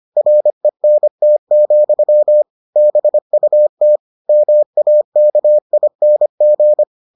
Morse code, heard as a keyed tone, about 25 wpm, 600 Hz. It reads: RENT, BUT MAKING